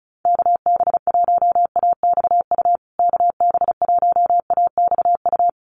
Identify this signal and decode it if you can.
Morse code, heard as a keyed tone, about 34 wpm, 700 Hz. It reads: KB1AXU KB1AXU
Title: USA Call Signs Repeated 3x 35wpm